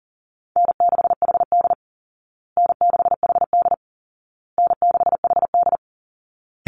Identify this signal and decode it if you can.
Morse code, heard as a keyed tone, about 40 wpm, 700 Hz. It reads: N6HD N6HD N6HD
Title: Call Signs Repeated 2x - 40wpm